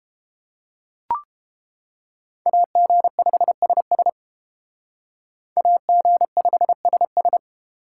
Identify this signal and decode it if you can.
Morse code, heard as a keyed tone, about 28 wpm, 700 Hz. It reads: AG5SS AG5SS